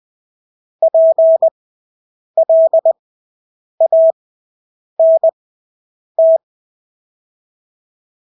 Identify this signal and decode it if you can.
Morse code, heard as a keyed tone, about 20 wpm, spaced as for 8 wpm, 650 Hz. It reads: PLANT
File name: Engelske ord 20/8 wpm